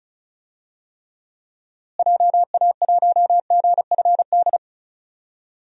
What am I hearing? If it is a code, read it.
Morse code, heard as a keyed tone, about 36 wpm, 700 Hz. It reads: JA1GFD
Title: DXCC-100 Call Signs - 6 Characters - Repeated 3x - 35wpm